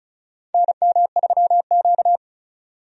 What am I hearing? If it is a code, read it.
Morse code, heard as a keyed tone, about 36 wpm, 700 Hz. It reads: NM3Q
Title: USA Call Signs - 4 Characters - 35wpm